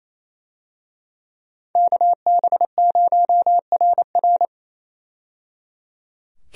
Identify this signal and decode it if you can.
Morse code, heard as a keyed tone, about 27 wpm, 700 Hz. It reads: KB0RR
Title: USA Call Signs - 5 Characters - 28wpm